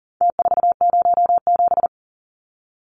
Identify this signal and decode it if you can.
Morse code, heard as a keyed tone, about 40 wpm, 700 Hz. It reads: T407
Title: Numbers - Contest Style - 4-Digit 40wpm